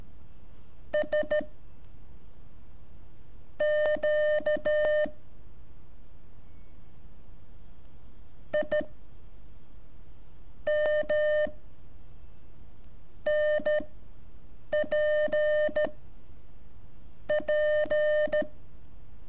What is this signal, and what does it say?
Morse code, heard as a keyed tone, about 13 wpm, spaced as for 7 wpm, 600 Hz. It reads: S Q I M NP P
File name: curso / ejercicio 22